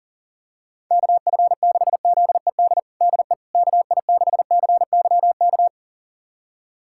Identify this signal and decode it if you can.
Morse code, heard as a keyed tone, about 40 wpm, 700 Hz. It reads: KF6ZED DE KI6CYK
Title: USA Callsign DE Callsign - 6 Characters - 40wpm